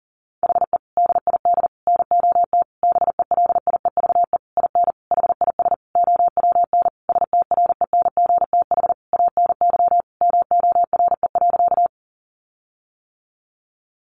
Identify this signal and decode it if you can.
Morse code, heard as a keyed tone, about 40 wpm, 700 Hz. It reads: HE DID NOT BELIEVE IN HIS OWN STRENGTH ANY MORE.
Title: Sentences from Top 1300 Words 40wpm